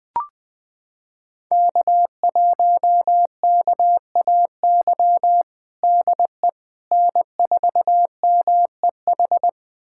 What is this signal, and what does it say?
Morse code, heard as a keyed tone, about 20 wpm, 700 Hz. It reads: K1KAY DE N4MEH
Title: USA Callsign DE Callsign - 5 Characters - 20wpm